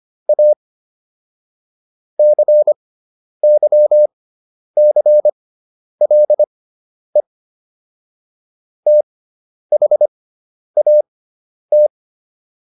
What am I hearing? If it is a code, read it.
Morse code, heard as a keyed tone, about 25 wpm, spaced as for 10 wpm, 600 Hz. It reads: A CYCLE THAT